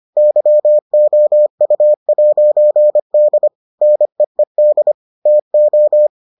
Morse code, heard as a keyed tone, about 25 wpm, 600 Hz. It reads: YOU'D NEED TO